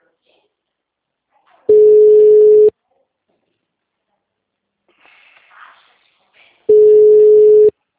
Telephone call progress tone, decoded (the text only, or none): ringback tone